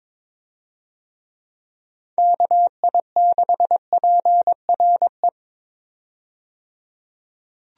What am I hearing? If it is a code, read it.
Morse code, heard as a keyed tone, about 22 wpm, 700 Hz. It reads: KI6PRE
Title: USA Call Signs - 6 Characters - 22wpm